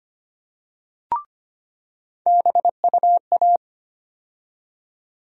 Morse code, heard as a keyed tone, about 25 wpm, 700 Hz. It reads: BUA
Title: USA Call Sign Postfixes - 3 Letters 25wpm